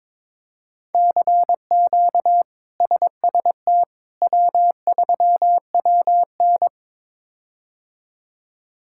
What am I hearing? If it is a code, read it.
Morse code, heard as a keyed tone, about 22 wpm, 700 Hz. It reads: CQ SST W3WN
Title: K1USN SST QSO Practice - 22wpm